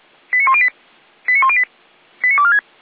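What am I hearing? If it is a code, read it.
Five-tone selective call (ZVEI), heard as quick runs of five tones, tones of about 80 ms. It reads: 89108 89108 89246
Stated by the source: Five-tone ZVEI selective call received with an SDR.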